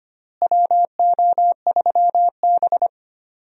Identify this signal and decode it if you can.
Morse code, heard as a keyed tone, about 25 wpm, 700 Hz. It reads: WO3B
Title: USA Call Signs - 4 Characters - Repeated 3x - 25wpm